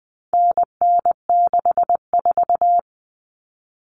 Morse code, heard as a keyed tone, about 20 wpm, 700 Hz. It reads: NN64
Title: Common Cut Numbers - 4-Digit 20wpm